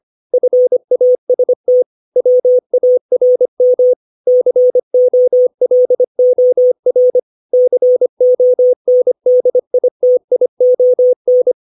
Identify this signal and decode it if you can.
Morse code, heard as a keyed tone, about 25 wpm, 500 Hz. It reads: FAST WARM COLOR CONDITION
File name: 632 words / 632 Words 25WPM